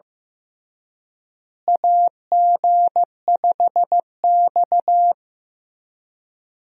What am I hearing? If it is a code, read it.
Morse code, heard as a keyed tone, about 15 wpm, 700 Hz. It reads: AG5X
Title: USA Call Signs - 4 Characters - Repeated 3x - 15wpm